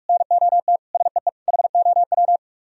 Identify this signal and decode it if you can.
Morse code, heard as a keyed tone, about 49 wpm, 700 Hz. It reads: NOT SEE HOW